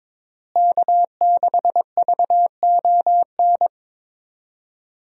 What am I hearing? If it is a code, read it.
Morse code, heard as a keyed tone, about 22 wpm, 700 Hz. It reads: K6VON
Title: USA Call Signs - 5 Characters - Repeated 3x - 22wpm